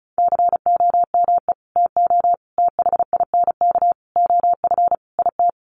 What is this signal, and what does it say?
Morse code, heard as a keyed tone, about 34 wpm, 700 Hz. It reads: COME TO THINK OF IT
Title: Sets of 5 Words Encore 35wpm